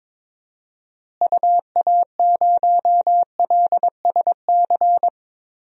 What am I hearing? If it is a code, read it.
Morse code, heard as a keyed tone, about 22 wpm, 700 Hz. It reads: UA0LSC
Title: DXCC-100 Call Signs - 6 Characters - Repeated 3x - 22wpm